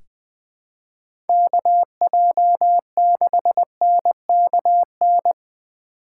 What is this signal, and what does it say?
Morse code, heard as a keyed tone, about 20 wpm, 700 Hz. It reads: KJ6NKN
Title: USA Call Signs - 8x Character Spacing - 20wpm